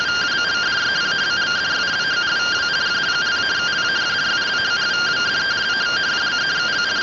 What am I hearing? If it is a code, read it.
Two-tone teleprinter signal, about 170 Hz shift.
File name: findings / CT AXIAL -256.gen-04 rtty rx